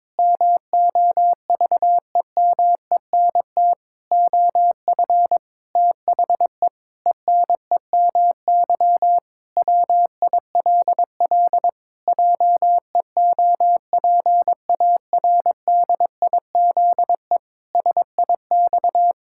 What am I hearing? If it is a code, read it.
Morse code, heard as a keyed tone, about 22 wpm, 700 Hz. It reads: MOVEMENT OF THE ENEMY WILL JEOPARDIZE SIX
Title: Pangram Challenge v2 - 22wpm